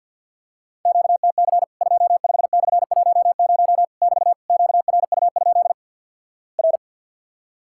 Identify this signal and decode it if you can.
Morse code, heard as a keyed tone, about 50 wpm, 700 Hz. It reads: QTC 25/10 <BT> QRU? R
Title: Worked All Europe DX Contest - QTC Exchange - 50wpm